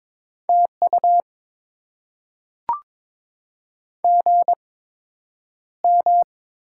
Morse code, heard as a keyed tone, about 22 wpm, spaced as for 9 wpm, 700 Hz. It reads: TU GM
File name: Top 20 QSO Elements - 8x Character Spacing - 22wpm